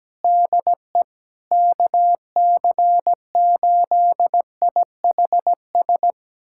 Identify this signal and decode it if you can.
Morse code, heard as a keyed tone, about 17 wpm, 700 Hz. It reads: DE KC8IHS
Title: USA Callsign DE Callsign 17wpm